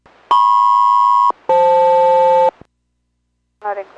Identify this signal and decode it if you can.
Aircraft SELCAL, heard as two pairs of tones, with speech. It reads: MP-FL